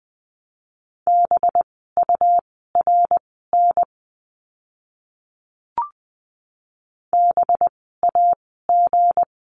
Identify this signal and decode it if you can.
Morse code, heard as a keyed tone, about 20 wpm, spaced as for 14 wpm, 700 Hz. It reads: BURN BAG
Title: 185 Lesson 30f - Letter b - Words 20wpm